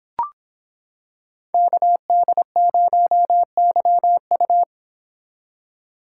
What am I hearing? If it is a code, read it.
Morse code, heard as a keyed tone, about 26 wpm, 700 Hz. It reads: KD0YU